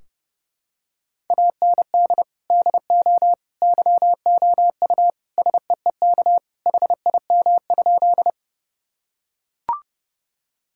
Morse code, heard as a keyed tone, about 30 wpm, 700 Hz. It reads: AND DO YOU SEEK HIM?
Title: Sentences from Top 1600 Words 30wpm